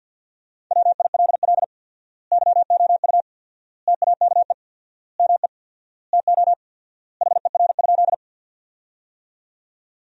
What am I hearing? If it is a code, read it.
Morse code, heard as a keyed tone, about 50 wpm, 700 Hz. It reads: WILL YOU TAKE ME TO HER?
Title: 4x Word-Spacing - Sentences from Top 200 Words - 50wpm